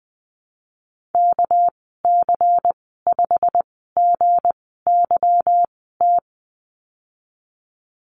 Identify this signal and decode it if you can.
Morse code, heard as a keyed tone, about 20 wpm, spaced as for 14 wpm, 700 Hz. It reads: KC5GYT
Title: 219 Lesson 35g - Number 0 - Call Signs 20wpm